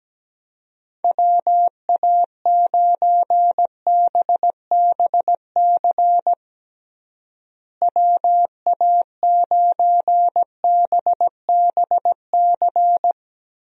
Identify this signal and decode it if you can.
Morse code, heard as a keyed tone, about 17 wpm, 700 Hz. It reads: WA9BBC WA9BBC
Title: USA Call Signs - 6 Characters - Repeated 2x - 17wpm